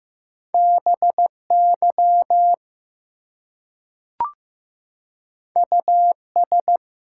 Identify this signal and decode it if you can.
Morse code, heard as a keyed tone, about 15 wpm, 700 Hz. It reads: BY US
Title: Two Letter Words - 15wpm